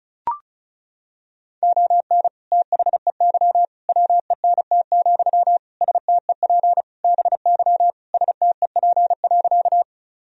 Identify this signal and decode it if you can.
Morse code, heard as a keyed tone, about 36 wpm, 700 Hz. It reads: ON THEY WENT, STEP BY STEP.
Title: Sentences from Top 1000 Words - Review Part 2 - 35wpm